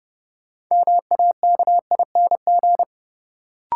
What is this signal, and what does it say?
Morse code, heard as a keyed tone, about 32 wpm, 700 Hz. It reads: MAKING
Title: Sentences from Top 1000 Words - Review Part 6 - New Word Drill - 30wpm